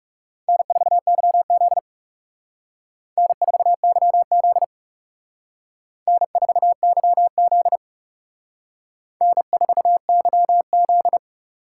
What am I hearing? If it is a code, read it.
Morse code, heard as a keyed tone, about 34 wpm, 700 Hz. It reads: N4YZ N4YZ N4YZ N4YZ
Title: Speed Racing - USA 1x2 Call Signs 30wpm